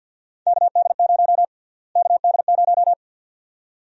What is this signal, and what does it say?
Morse code, heard as a keyed tone, about 49 wpm, 700 Hz. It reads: KD0 KD0
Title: USA Call Sign Prefixes - Repeated 2x - 50wpm